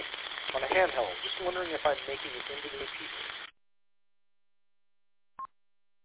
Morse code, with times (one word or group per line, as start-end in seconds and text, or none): none